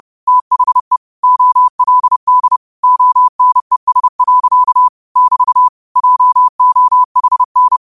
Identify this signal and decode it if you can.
Morse code, heard as a keyed tone, about 30 wpm, 1000 Hz. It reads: THE OLD ONES. <BT> JOHN